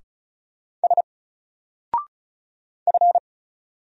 Morse code, heard as a keyed tone, about 36 wpm, 700 Hz. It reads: S F
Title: Single Character (Letter:Number:Punctuation:Prosign) 35wpm